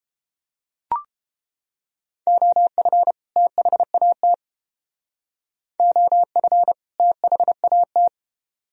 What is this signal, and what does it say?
Morse code, heard as a keyed tone, about 31 wpm, 700 Hz. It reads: OF THAT OF THAT